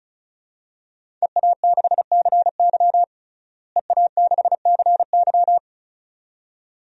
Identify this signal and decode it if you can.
Morse code, heard as a keyed tone, about 34 wpm, 700 Hz. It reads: EA6CY EA6CY
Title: DXCC-100 Call Signs - 5 Characters - Repeated 2x - 35wpm